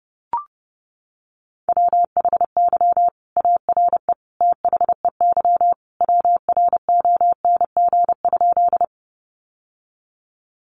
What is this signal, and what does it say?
Morse code, heard as a keyed tone, about 30 wpm, 700 Hz. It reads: WHY ARE THEY WRONG?
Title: Sentences from Top 900 Words 30wpm